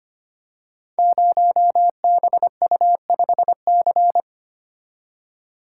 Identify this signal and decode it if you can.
Morse code, heard as a keyed tone, about 25 wpm, 700 Hz. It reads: 0BU5C
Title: Code Groups - 5 Letter:Numbers - 25wpm